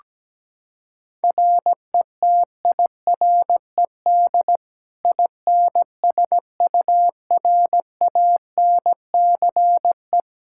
Morse code, heard as a keyed tone, about 17 wpm, 700 Hz. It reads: RETIRED INSURANCE
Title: K1USN SST QSO Practice 17wpm